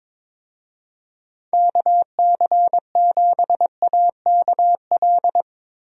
Morse code, heard as a keyed tone, about 22 wpm, 700 Hz. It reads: KC7AKL
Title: USA Call Signs - 6 Characters - Repeated 2x - 22wpm